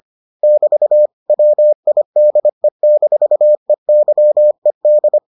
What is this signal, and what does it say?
Morse code, heard as a keyed tone, about 25 wpm, 600 Hz. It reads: <BT> WIDE-EYED